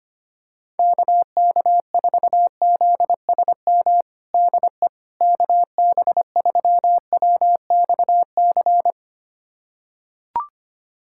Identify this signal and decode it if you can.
Morse code, heard as a keyed tone, about 25 wpm, 700 Hz. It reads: KK4ZSM DE KB3WXC
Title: USA Callsign DE Callsign - 6 Characters - 25wpm